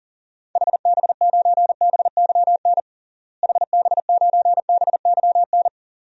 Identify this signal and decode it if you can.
Morse code, heard as a keyed tone, about 40 wpm, 700 Hz. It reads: HB9BYN HB9BYN
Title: DXCC-100 Call Signs - 6 Characters - Repeated 2x - 40wpm